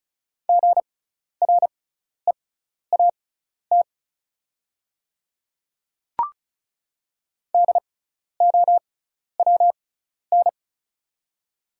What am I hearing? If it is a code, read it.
Morse code, heard as a keyed tone, about 36 wpm, spaced as for 12 wpm, 690 Hz. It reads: GREAT DOWN
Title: 6x Character Spacing - Top 200 Words - 35wpm